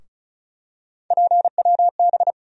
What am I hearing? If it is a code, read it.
Morse code, heard as a keyed tone, about 36 wpm, 700 Hz. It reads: PWB
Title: USA Call Sign Postfixes - 3 Letters 35wpm